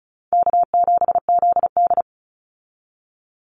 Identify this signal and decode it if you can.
Morse code, heard as a keyed tone, about 34 wpm, 700 Hz. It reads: K7ZD
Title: Easy Call Signs Repeated 3x - 35wpm